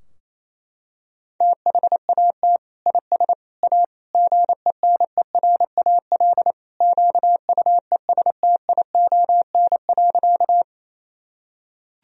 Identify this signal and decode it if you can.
Morse code, heard as a keyed tone, about 28 wpm, 700 Hz. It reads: THAT IS A GENERAL QUESTION.
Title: Sentences from Top 1000 Words - Review Part 3 - 28wpm